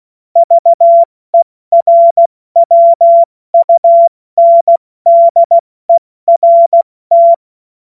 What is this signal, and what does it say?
Morse code, heard as a keyed tone, about 16 wpm, spaced as for 14 wpm, 680 Hz. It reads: VERWUNDERT
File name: Morsekurs / Witze / Witz 30. Dezember 2025 70 F80